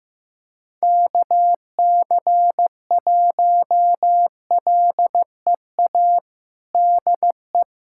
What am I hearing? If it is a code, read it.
Morse code, heard as a keyed tone, about 15 wpm, 700 Hz. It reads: KC1LEA DE
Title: USA Callsign DE Callsign 15wpm